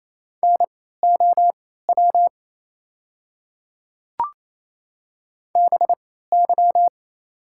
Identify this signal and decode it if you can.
Morse code, heard as a keyed tone, about 28 wpm, spaced as for 16 wpm, 690 Hz. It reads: NOW BY
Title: 3x Character Spacing - Top 100 Words - 28wpm